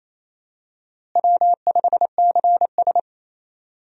Morse code, heard as a keyed tone, about 28 wpm, 700 Hz. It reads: W5CS
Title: USA Call Signs - 4 Characters - Repeated 2x - 28wpm